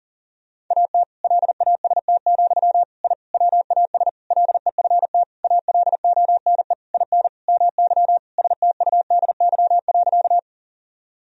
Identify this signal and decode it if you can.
Morse code, heard as a keyed tone, about 40 wpm, 700 Hz. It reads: AT LAST, I WAS LEFT ALONE IN MY STUDY.
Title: Sentences from Top 1000 Words - Review Part 1 - 40wpm